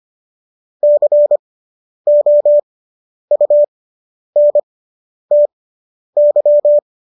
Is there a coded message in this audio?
Yes: Morse code from 0.8 to 6.8 s, heard as a keyed tone, about 25 wpm, spaced as for 10 wpm, 590 Hz. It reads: COUNTY